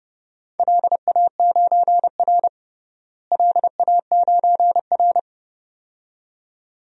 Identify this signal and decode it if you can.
Morse code, heard as a keyed tone, about 30 wpm, 700 Hz. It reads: LA9R LA9R
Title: DXCC-100 Call Signs - 4 Characters - Repeated 2x - 30wpm